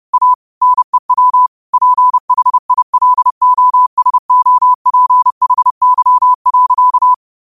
Morse code, heard as a keyed tone, about 30 wpm, 1000 Hz. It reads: A NEW PHILOSOPHY.
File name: qotdfiles / 30wpm